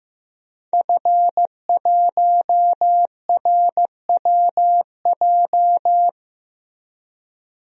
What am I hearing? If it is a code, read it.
Morse code, heard as a keyed tone, about 15 wpm, 700 Hz. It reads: F1RWJ
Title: DXCC-100 Call Signs - 5 Characters - Repeated 3x - 15wpm